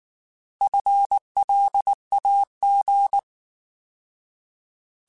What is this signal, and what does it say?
Morse code, heard as a keyed tone, about 19 wpm, 800 Hz. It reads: FLAG
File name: Alphabetical / Alphabetical mp3s / Fs @18